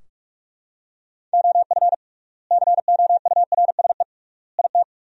Morse code, heard as a keyed tone, about 41 wpm, 700 Hz. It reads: OF COURSE IT
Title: 3x Word-Spacing - Sentences from Top 500 Words - 45wpm